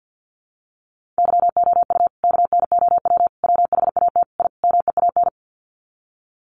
Morse code, heard as a keyed tone, about 50 wpm, 700 Hz. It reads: YOU KNOW WHAT I MEAN
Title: Sets of 5 Words 50wpm